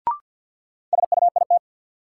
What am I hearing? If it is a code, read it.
Morse code, heard as a keyed tone, about 52 wpm, 700 Hz. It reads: SUIT
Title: Sentences from Top 1600 Words - New Word Drill - 50wpm